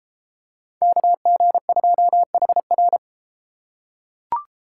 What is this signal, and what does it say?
Morse code, heard as a keyed tone, about 30 wpm, 700 Hz. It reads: KG2HR